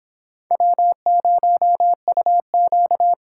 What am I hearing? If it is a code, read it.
Morse code, heard as a keyed tone, about 25 wpm, 700 Hz. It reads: W0UQ